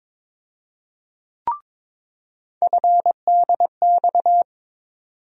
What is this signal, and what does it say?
Morse code, heard as a keyed tone, about 22 wpm, 700 Hz. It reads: FDX
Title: USA Call Sign Postfixes - 3 Letters 22wpm